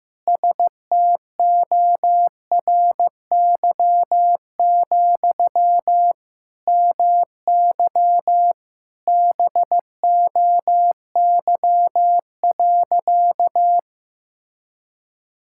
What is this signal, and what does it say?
Morse code, heard as a keyed tone, about 15 wpm, 700 Hz. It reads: STORY, MY BOY.
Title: Sentences from Top 1000 Words - Review Part 6 - 15wpm